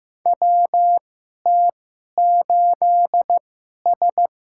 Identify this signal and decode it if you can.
Morse code, heard as a keyed tone, about 15 wpm, spaced as for 11 wpm, 700 Hz. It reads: WT8S
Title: USA Call Signs - 4 Characters - 2x Character Spacing - 15wpm